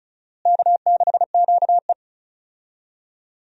Morse code, heard as a keyed tone, about 36 wpm, 700 Hz. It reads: K6QE
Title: USA Call Signs - 4 Characters - Repeated 3x - 35wpm